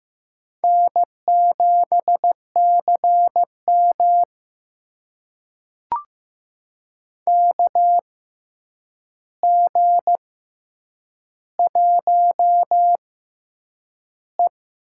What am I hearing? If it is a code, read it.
Morse code, heard as a keyed tone, about 15 wpm, spaced as for 8 wpm, 700 Hz. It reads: N7CM KG1E
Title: USA Call Signs - 6x Character Spacing - 15wpm